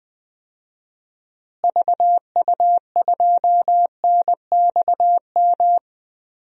Morse code, heard as a keyed tone, about 20 wpm, 700 Hz. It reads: VU2NXM
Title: DXCC-100 Call Signs - 6 Characters - Repeated 3x - 20wpm